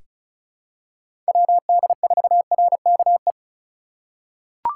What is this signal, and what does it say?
Morse code, heard as a keyed tone, about 34 wpm, 700 Hz. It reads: WD4RKE
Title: USA Call Signs - 6 Characters - 2x Character Spacing - 35wpm